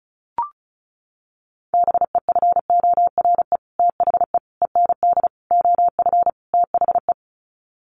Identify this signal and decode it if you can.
Morse code, heard as a keyed tone, about 34 wpm, 700 Hz. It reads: BEFORE THE END OF THE
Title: Sets of 5 Words Encore 35wpm